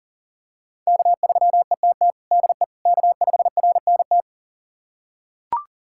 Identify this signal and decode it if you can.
Morse code, heard as a keyed tone, about 40 wpm, 700 Hz. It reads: K3ETT DE K5RNT
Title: USA Callsign DE Callsign - 5 Characters - 40wpm